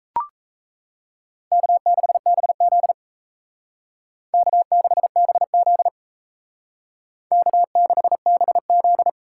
Morse code, heard as a keyed tone, about 37 wpm, 700 Hz. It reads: K6BZ K6BZ K6BZ